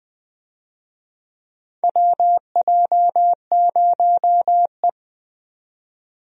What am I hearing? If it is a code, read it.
Morse code, heard as a keyed tone, about 20 wpm, 700 Hz. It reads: WJ0E
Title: USA Call Signs - 4 Characters - 20wpm